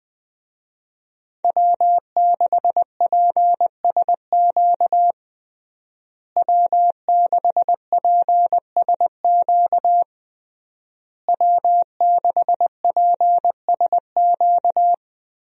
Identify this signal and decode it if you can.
Morse code, heard as a keyed tone, about 20 wpm, 700 Hz. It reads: W6PSQ W6PSQ W6PSQ
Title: USA Call Signs - 5 Characters - Repeated 3x - 20wpm